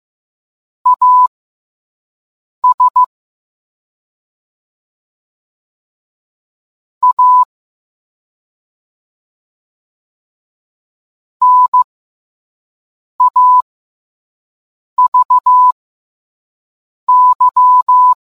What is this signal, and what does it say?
Morse code, heard as a keyed tone, about 15 wpm, spaced as for 5 wpm, 1000 Hz. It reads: AS A NAVY